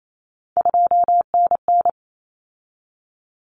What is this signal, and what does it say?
Morse code, heard as a keyed tone, about 28 wpm, 700 Hz. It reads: 2NN
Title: Common Cut Numbers - 3-Digit 28wpm